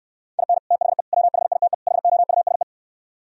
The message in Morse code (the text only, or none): IN THE WHITE HOUSE